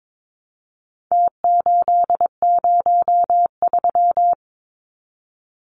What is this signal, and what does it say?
Morse code, heard as a keyed tone, about 22 wpm, 700 Hz. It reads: T803
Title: Numbers - Contest Style - 4-Digit 22wpm